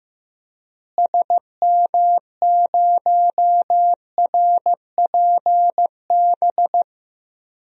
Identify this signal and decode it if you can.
Morse code, heard as a keyed tone, about 15 wpm, 700 Hz. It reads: SM0RPB
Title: DXCC-100 Call Signs - 6 Characters - Repeated 3x - 15wpm